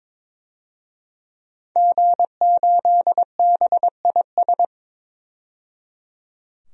Morse code, heard as a keyed tone, about 22 wpm, 700 Hz. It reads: G8BIS
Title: DXCC-100 Call Signs - 5 Characters - 22wpm